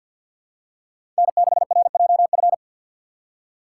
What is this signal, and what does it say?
Morse code, heard as a keyed tone, about 53 wpm, 700 Hz. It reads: N6RJF
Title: USA Call Signs - 5 Characters - 4x Character Spacing - 50wpm